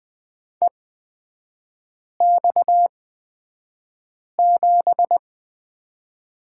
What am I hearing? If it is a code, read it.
Morse code, heard as a keyed tone, about 20 wpm, spaced as for 5 wpm, 700 Hz. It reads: EX7
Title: 255 Lesson 41e - Period - Three Characters 20wpm